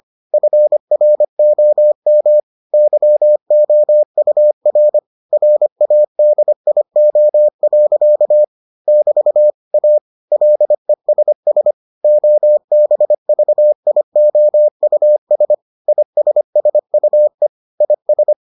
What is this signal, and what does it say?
Morse code, heard as a keyed tone, about 25 wpm, 600 Hz. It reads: FROM YOUR RADIO. <BT> A LESS OBVIOUS ISSUE IS